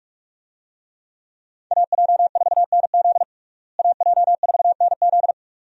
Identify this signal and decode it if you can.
Morse code, heard as a keyed tone, about 41 wpm, 700 Hz. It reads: AJ4NZ AJ4NZ
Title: USA Call Signs - 5 Characters - Repeated 2x - 45wpm